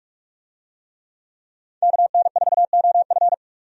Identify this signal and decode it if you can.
Morse code, heard as a keyed tone, about 43 wpm, 700 Hz. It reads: KN4OF
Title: USA Call Signs - 5 Characters - 45wpm